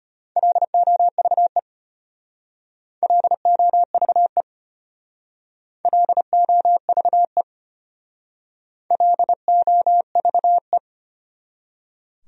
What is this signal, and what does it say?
Morse code, heard as a keyed tone, about 33 wpm, 700 Hz. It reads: LOVE LOVE LOVE LOVE